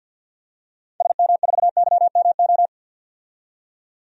Morse code, heard as a keyed tone, about 52 wpm, 700 Hz. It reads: SM4YMO
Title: DXCC-100 Call Signs - 6 Characters - 50wpm